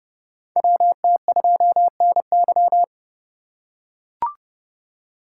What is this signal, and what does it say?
Morse code, heard as a keyed tone, about 30 wpm, 700 Hz. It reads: WT2NY
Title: USA Call Signs - 4x Character Spacing - 30wpm